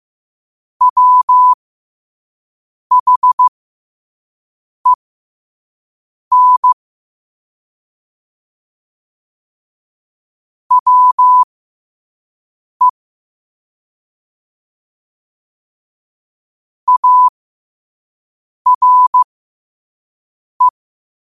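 Morse code, heard as a keyed tone, about 15 wpm, spaced as for 5 wpm, 1000 Hz. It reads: WHEN WE ARE
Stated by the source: Morse code sent at 5 wpm.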